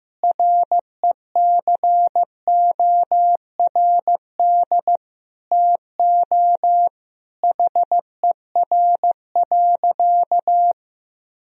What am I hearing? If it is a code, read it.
Morse code, heard as a keyed tone, about 16 wpm, 700 Hz. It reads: RECORD TO HER.
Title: Sentences from Top 1000 Words - Review Part 3 - 15wpm